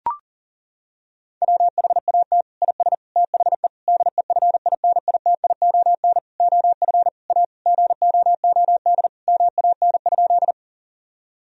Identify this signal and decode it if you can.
Morse code, heard as a keyed tone, about 40 wpm, 700 Hz. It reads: WHAT IS THE DEFINITION OF A GOOD MAN?
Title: Sentences from Top 1000 Words - Review Part 4 - 40wpm